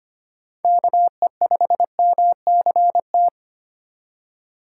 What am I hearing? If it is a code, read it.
Morse code, heard as a keyed tone, about 25 wpm, 700 Hz. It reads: KE5MCT
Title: USA Call Signs - 6 Characters - 4x Character Spacing - 25wpm